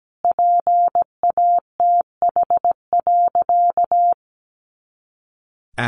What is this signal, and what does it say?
Morse code, heard as a keyed tone, about 17 wpm, 700 Hz. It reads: PATH.
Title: A Sound Of Thunder 17wpm